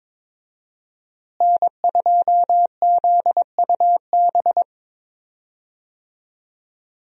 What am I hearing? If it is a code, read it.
Morse code, heard as a keyed tone, about 22 wpm, 700 Hz. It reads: N2ZUB
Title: USA Call Signs - 5 Characters - Repeated 3x - 22wpm